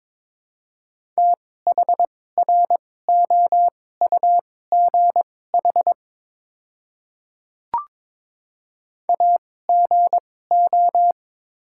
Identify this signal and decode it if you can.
Morse code, heard as a keyed tone, about 22 wpm, spaced as for 16 wpm, 690 Hz. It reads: THROUGH AGO
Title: 2x Character Spacing - Top 500 Words - 22wpm